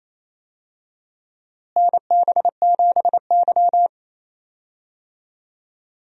Morse code, heard as a keyed tone, about 27 wpm, 700 Hz. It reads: NB7Y
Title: USA Call Signs - 4 Characters - Repeated 2x - 28wpm